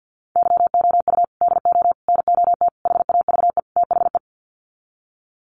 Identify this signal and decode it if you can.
Morse code, heard as a keyed tone, about 50 wpm, 700 Hz. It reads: YOU DO NOT HAVE THE
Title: Sets of 5 Words 50wpm